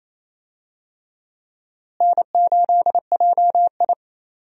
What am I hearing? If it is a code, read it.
Morse code, heard as a keyed tone, about 28 wpm, 700 Hz. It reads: N8JI
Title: USA Call Signs - 4 Characters - Repeated 3x - 28wpm